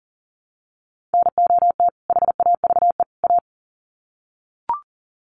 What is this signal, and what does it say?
Morse code, heard as a keyed tone, about 38 wpm, 700 Hz. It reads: NOT HAVE A
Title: Sets of 3 Words 40wpm